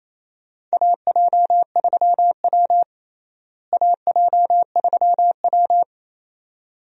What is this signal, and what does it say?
Morse code, heard as a keyed tone, about 28 wpm, 700 Hz. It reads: AJ3W AJ3W
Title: USA Call Signs - 4 Characters - Repeated 2x - 28wpm